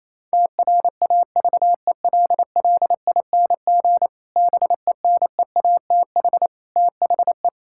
Morse code, heard as a keyed tone, about 28 wpm, 700 Hz. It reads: TRAVELLING BENEATH THE
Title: Pangram Challenge v2 - 28wpm